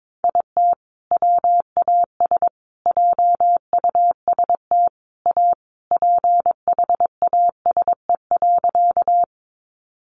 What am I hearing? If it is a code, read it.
Morse code, heard as a keyed tone, about 22 wpm, 700 Hz. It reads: IT WAS JUST A PHASE.
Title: Sentences from Top 1200 Words 22wpm